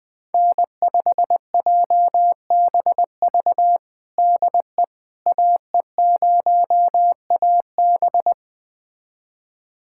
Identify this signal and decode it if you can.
Morse code, heard as a keyed tone, about 21 wpm, 700 Hz. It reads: N5JBV DE AE0AB
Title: USA Callsign DE Callsign - 5 Characters - 20wpm